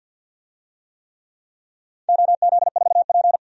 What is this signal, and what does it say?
Morse code, heard as a keyed tone, about 52 wpm, 700 Hz. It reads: OZ4P
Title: DXCC-100 Call Signs - 4 Characters - 50wpm